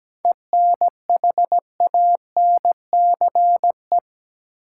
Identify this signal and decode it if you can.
Morse code, heard as a keyed tone, about 17 wpm, 700 Hz. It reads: ENHANCE
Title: Seven Letter Words - 17wpm